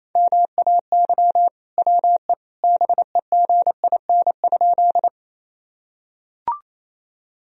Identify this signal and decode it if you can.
Morse code, heard as a keyed tone, about 28 wpm, 700 Hz. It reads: MAY WE BEGIN?
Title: Sentences from Top 1000 Words - Review Part 1 - 28wpm